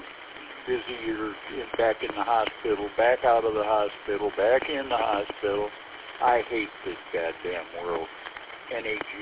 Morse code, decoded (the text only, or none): none